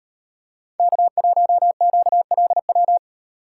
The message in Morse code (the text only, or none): K1QLW